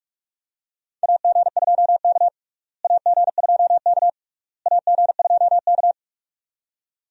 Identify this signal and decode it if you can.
Morse code, heard as a keyed tone, about 46 wpm, 700 Hz. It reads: AG2K AG2K AG2K
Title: USA Call Signs - 4 Characters - Repeated 3x - 45wpm